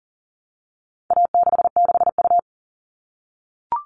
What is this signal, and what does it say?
Morse code, heard as a keyed tone, about 40 wpm, 700 Hz. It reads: A66U
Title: All Cut Numbers - 4-Digit 40wpm